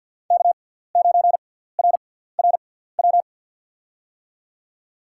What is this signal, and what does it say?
Morse code, heard as a keyed tone, about 53 wpm, spaced as for 17 wpm, 700 Hz. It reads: K9RRW
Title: USA Call Signs - 5 Characters - 6x Character Spacing - 50wpm